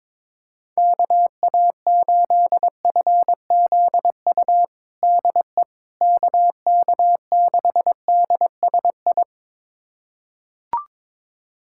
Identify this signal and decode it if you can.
Morse code, heard as a keyed tone, about 22 wpm, 700 Hz. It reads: KA8FZU DE KK6DSI
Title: USA Callsign DE Callsign - 6 Characters - 22wpm